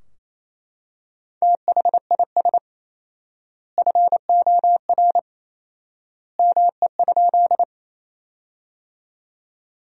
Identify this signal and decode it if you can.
Morse code, heard as a keyed tone, about 28 wpm, 700 Hz. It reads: THIS FOR ME?
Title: 4x Word-Spacing - Sentences from Top 200 Words - 28wpm